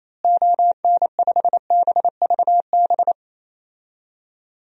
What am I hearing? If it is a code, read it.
Morse code, heard as a keyed tone, about 28 wpm, 700 Hz. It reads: ON5BVB
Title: DXCC-100 Call Signs - 6 Characters - Repeated 3x - 28wpm